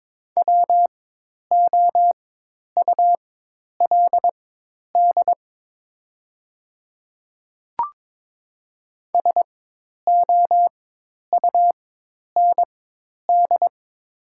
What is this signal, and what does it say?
Morse code, heard as a keyed tone, about 22 wpm, spaced as for 10 wpm, 700 Hz. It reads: WOULD SOUND
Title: 4x Character Spacing - Top 100 Words - 22wpm